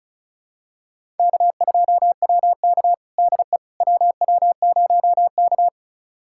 Morse code, heard as a keyed tone, about 36 wpm, 700 Hz. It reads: K2WK DE WW0K
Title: USA Callsign DE Callsign - 4 Characters - 35wpm